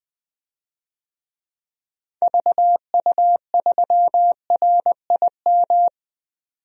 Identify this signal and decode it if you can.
Morse code, heard as a keyed tone, about 20 wpm, 700 Hz. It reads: VU3RIM
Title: DXCC-100 Call Signs - 6 Characters - 20wpm